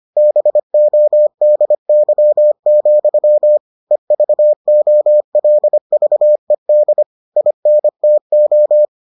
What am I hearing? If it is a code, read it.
Morse code, heard as a keyed tone, about 25 wpm, 600 Hz. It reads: BODY, EVOLVED INTO